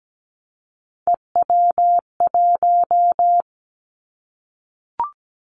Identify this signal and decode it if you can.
Morse code, heard as a keyed tone, about 17 wpm, 700 Hz. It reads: EW1
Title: Callsign Prefixes for Top 50 DXCC - 17wpm